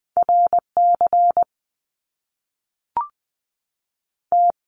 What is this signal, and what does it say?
Morse code, heard as a keyed tone, about 20 wpm, 700 Hz. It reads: R C T
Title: Top Two Letter Combinations in Words 20wpm